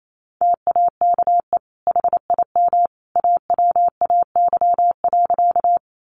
Morse code, heard as a keyed tone, about 28 wpm, 700 Hz. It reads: TAKE HIM AWAY.
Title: Sentences from Top 600 Words 28wpm